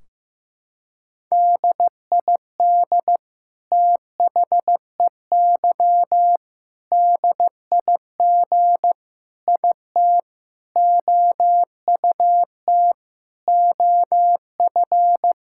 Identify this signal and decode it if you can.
Morse code, heard as a keyed tone, about 16 wpm, 700 Hz. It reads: DID THEY DIG IT OUT OF
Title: Sentences from Top 1600 Words 15wpm